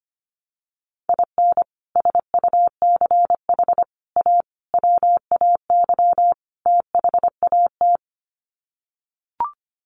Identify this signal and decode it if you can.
Morse code, heard as a keyed tone, about 25 wpm, 700 Hz. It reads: IN SUCH A WAY THAT
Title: Sets of 5 Words 25wpm